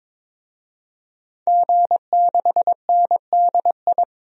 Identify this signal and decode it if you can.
Morse code, heard as a keyed tone, about 22 wpm, 700 Hz. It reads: G6NDI
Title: DXCC-100 Call Signs - 5 Characters - 22wpm